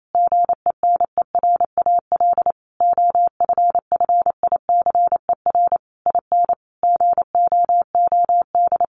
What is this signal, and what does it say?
Morse code, heard as a keyed tone, about 28 wpm, 700 Hz. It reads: GENERAL OFFICER IN GOOD
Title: Sentences from Top 1200 Words 28wpm